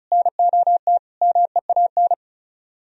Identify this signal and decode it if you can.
Morse code, heard as a keyed tone, about 42 wpm, 700 Hz. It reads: NOT MEAN
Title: Speed-Racing - Sets of 4 Words 35wpm